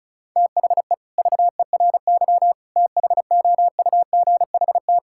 Morse code, heard as a keyed tone, about 36 wpm, 700 Hz. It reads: THE VERY THOUGHT
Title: Sentences from Top 1000 Words - Review Part 5 - 35wpm